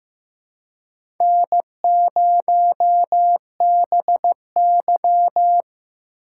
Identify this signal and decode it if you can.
Morse code, heard as a keyed tone, about 15 wpm, 700 Hz. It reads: N0BY
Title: USA Call Signs - 4 Characters - 2x Character Spacing - 15wpm